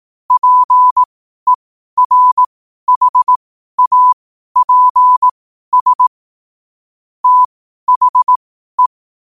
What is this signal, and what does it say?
Morse code, heard as a keyed tone, about 18 wpm, spaced as for 12 wpm, 1000 Hz. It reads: PERHAPS THE
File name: qotdfiles / 12wpm